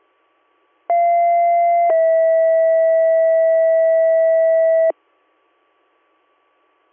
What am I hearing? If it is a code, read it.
Two-tone page, heard as a one-second tone, then a three-second tone, about 690 then 650 Hz. It is heard from 0.9 to 4.9 s.